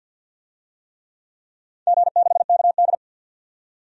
Morse code, heard as a keyed tone, about 51 wpm, 700 Hz. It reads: G6KD
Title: DXCC-100 Call Signs - 4 Characters - 50wpm